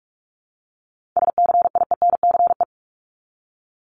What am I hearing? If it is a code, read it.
Morse code, heard as a keyed tone, about 46 wpm, 700 Hz. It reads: SCIENCE
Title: Multi-letter Phonemes Repeated 3x 45wpm